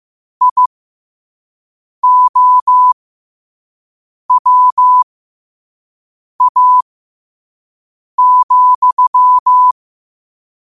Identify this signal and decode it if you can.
Morse code, heard as a keyed tone, about 15 wpm, spaced as for 5 wpm, 1000 Hz. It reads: IOWA,
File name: qotdfiles / 5wpm